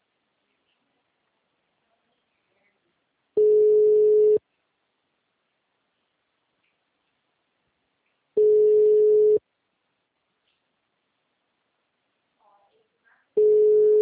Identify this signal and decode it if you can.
Ringback tone (Europe, 425 Hz), with speech.